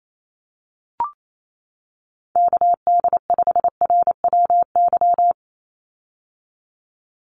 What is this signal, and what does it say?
Morse code, heard as a keyed tone, about 28 wpm, 700 Hz. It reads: KD5RWY
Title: USA Call Signs 28wpm